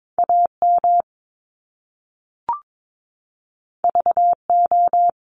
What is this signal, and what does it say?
Morse code, heard as a keyed tone, about 22 wpm, 700 Hz. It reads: A M V O
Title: Top Two Letter Combinations in Words 22wpm